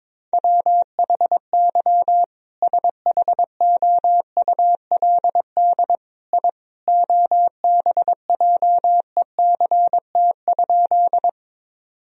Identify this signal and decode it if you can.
Morse code, heard as a keyed tone, about 21 wpm, 700 Hz. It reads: WHY SHOULD I OBJECT?
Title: Sentences from Top 1000 Words - Review Part 3 - 22wpm